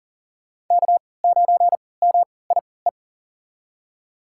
Morse code, heard as a keyed tone, about 40 wpm, spaced as for 23 wpm, 700 Hz. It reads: K9MIE
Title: USA Call Signs - 5 Characters - 3x Character Spacing - 40wpm